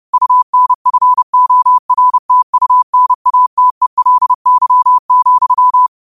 Morse code, heard as a keyed tone, about 30 wpm, 1000 Hz. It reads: UNFORTUNATELY,